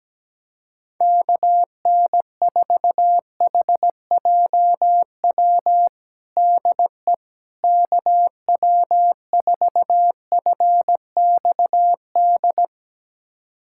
Morse code, heard as a keyed tone, about 17 wpm, 700 Hz. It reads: KN4HJW DE KW4FXD
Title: USA Callsign DE Callsign - 6 Characters - 17wpm